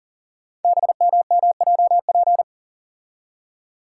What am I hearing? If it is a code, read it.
Morse code, heard as a keyed tone, about 38 wpm, 700 Hz. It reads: BMMJP
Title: Code Groups - 5 Letter:Numbers - 40wpm